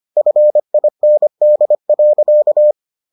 Morse code, heard as a keyed tone, about 25 wpm, 600 Hz. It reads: FIND.